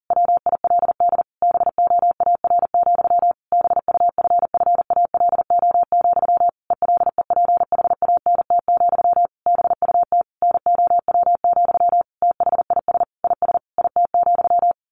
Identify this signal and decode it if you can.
Morse code, heard as a keyed tone, about 40 wpm, 700 Hz. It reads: WILD BOAR, BUFFALO, ELEPHANT, BUT NOW, THIS IS IT,
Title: A Sound Of Thunder 40wpm